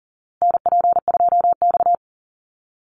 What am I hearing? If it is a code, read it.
Morse code, heard as a keyed tone, about 40 wpm, 700 Hz. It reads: NP2X
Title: Call Signs Repeated 2x - 40wpm